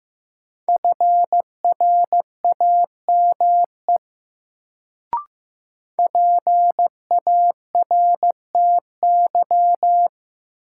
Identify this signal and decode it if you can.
Morse code, heard as a keyed tone, about 15 wpm, 700 Hz. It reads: FRAME PARTY
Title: Sentences from Top 1400 Words - New Word Drill Alt - 15wpm